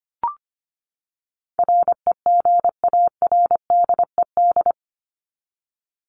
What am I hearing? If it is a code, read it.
Morse code, heard as a keyed tone, about 25 wpm, 700 Hz. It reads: REGARDED
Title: ED Suffix - Top 100 Words with Base Words 25wpm